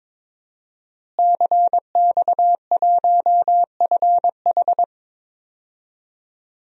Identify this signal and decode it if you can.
Morse code, heard as a keyed tone, about 23 wpm, 700 Hz. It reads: CX1FH
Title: DXCC-100 Call Signs - 5 Characters - Repeated 2x - 22wpm